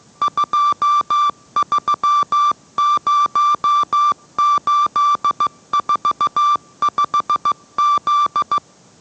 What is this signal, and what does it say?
Morse code, heard as a keyed tone, about 15 wpm, 1170 Hz. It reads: 230845Z